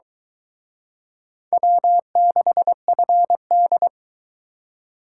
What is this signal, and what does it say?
Morse code, heard as a keyed tone, about 23 wpm, 700 Hz. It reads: W6FD